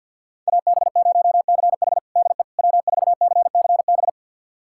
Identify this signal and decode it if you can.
Morse code, heard as a keyed tone, about 51 wpm, 700 Hz. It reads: AB0CH DE W4CCB
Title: USA Callsign DE Callsign - 5 Characters - 50wpm